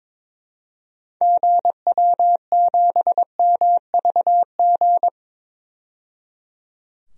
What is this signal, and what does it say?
Morse code, heard as a keyed tone, about 22 wpm, 700 Hz. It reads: GW7MVG
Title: DXCC-100 Call Signs - 6 Characters - 22wpm